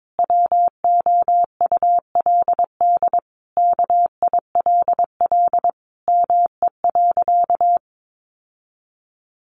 Morse code, heard as a keyed tone, about 22 wpm, 700 Hz. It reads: WOULD KILL ME.
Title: Sentences from Top 1400 Words 22wpm